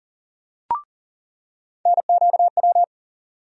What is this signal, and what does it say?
Morse code, heard as a keyed tone, about 40 wpm, 700 Hz. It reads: NQW
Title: USA Call Sign Postfixes - 3 Letters 40wpm